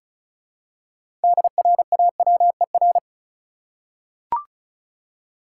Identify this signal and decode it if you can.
Morse code, heard as a keyed tone, about 34 wpm, 700 Hz. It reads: DRAWER
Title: Six Letter Words - 35wpm